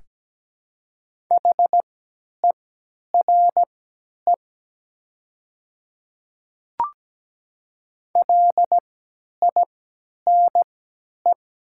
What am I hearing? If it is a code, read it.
Morse code, heard as a keyed tone, about 17 wpm, spaced as for 10 wpm, 700 Hz. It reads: HERE LINE
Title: 3x Character Spacing - Top 300 Words - 17wpm